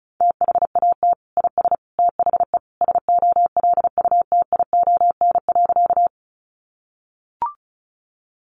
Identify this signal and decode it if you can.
Morse code, heard as a keyed tone, about 34 wpm, 700 Hz. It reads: THAT IS THE SOLUTION.
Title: Sentences from Top 1300 Words 35wpm